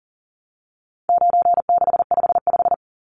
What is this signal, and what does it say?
Morse code, heard as a keyed tone, about 40 wpm, 700 Hz. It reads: 9655
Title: Numbers - 4-Digit 40wpm